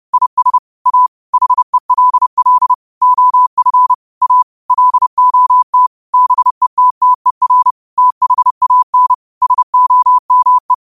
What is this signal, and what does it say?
Morse code, heard as a keyed tone, about 30 wpm, 1000 Hz. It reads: IS A HELL OF A LOT BETTER THAN SOME
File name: qotdfiles / 30wpm